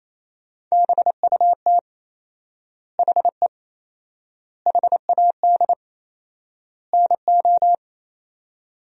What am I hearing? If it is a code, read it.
Morse code, heard as a keyed tone, about 28 wpm, 700 Hz. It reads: BUT HE HAD NO
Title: 4x Word-Spacing - Sentences from Top 100 Words - 28wpm